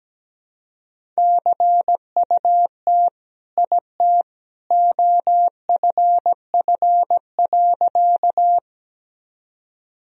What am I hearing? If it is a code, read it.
Morse code, heard as a keyed tone, about 17 wpm, 700 Hz. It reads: CUT IT OFF.
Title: Sentences from Top 1000 Words - Review Part 1 - 17wpm